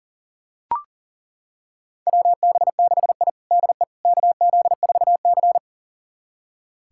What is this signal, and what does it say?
Morse code, heard as a keyed tone, about 40 wpm, 700 Hz. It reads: WB6I DE KZ4C
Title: USA Callsign DE Callsign - 4 Characters - 40wpm